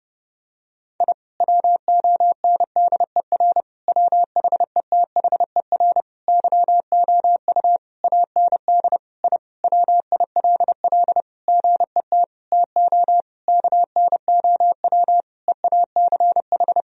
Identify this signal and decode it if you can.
Morse code, heard as a keyed tone, about 30 wpm, 700 Hz. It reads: I WONDER WHETHER YOU AND I WILL GET TO KNOW EACH
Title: Sentences from Top 1000 Words - Review Part 3 - 30wpm